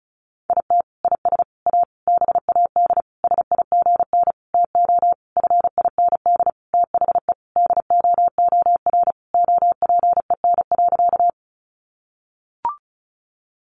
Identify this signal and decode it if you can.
Morse code, heard as a keyed tone, about 34 wpm, 700 Hz. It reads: IT IS A BAD SIGN TO FIND THE DOOR OPEN.
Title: Sentences from Top 1000 Words 35wpm